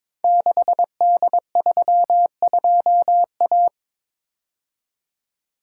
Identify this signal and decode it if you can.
Morse code, heard as a keyed tone, about 22 wpm, 700 Hz. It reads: 6D32A
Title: Code Groups - 5 Letter:Numbers - 22wpm